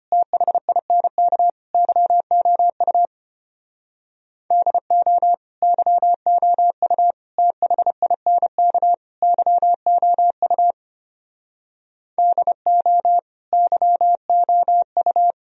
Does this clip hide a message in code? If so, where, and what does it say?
Yes: Morse code from 0.1 to 15.3 s, heard as a keyed tone, about 32 wpm, 700 Hz. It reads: THINK YOU DO YOU THINK YOU DO YOU